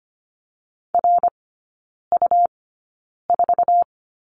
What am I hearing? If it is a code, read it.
Morse code, heard as a keyed tone, about 25 wpm, spaced as for 9 wpm, 700 Hz. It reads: RU4
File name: Lesson 17e - Letter u - Three Characters 25wpm